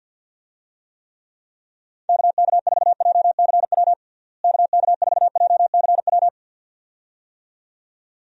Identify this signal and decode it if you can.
Morse code, heard as a keyed tone, about 53 wpm, 700 Hz. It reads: KK4JCW KK4JCW
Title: USA Call Signs - 6 Characters - Repeated 2x - 50wpm